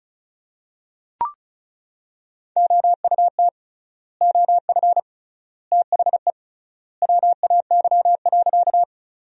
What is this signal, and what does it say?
Morse code, heard as a keyed tone, about 34 wpm, 700 Hz. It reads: OUT OF THE WAY.
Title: 3x Word-Spacing - Sentences from Top 100 Words - 35wpm